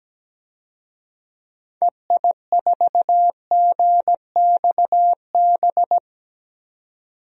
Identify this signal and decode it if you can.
Morse code, heard as a keyed tone, about 17 wpm, 700 Hz. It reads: EI4GXB
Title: DXCC-100 Call Signs - 6 Characters - 17wpm